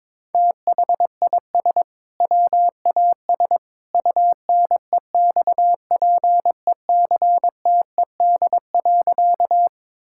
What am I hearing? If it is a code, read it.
Morse code, heard as a keyed tone, about 22 wpm, 700 Hz. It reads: THIS WAS UNEXPECTED.
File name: practice / 22wpm / Sentences with 2-Word Phrases - 22wpm